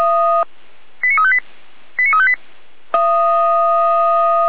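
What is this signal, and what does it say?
Five-tone selective call (ZVEI), heard as quick runs of five tones, tones of about 70 ms. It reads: 89358 89358
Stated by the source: ZVEI five-tone selective call received on SDR.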